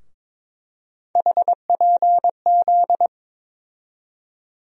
Morse code, heard as a keyed tone, about 22 wpm, 700 Hz. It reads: HPZ
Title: USA Call Sign Postfixes - 3 Letters 22wpm